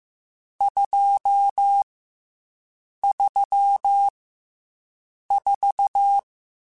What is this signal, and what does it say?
Morse code, heard as a keyed tone, about 15 wpm, 800 Hz. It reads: 2 3 4